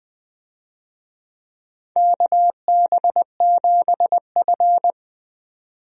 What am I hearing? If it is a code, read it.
Morse code, heard as a keyed tone, about 20 wpm, 700 Hz. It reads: KB7F
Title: USA Call Signs - 4 Characters - 20wpm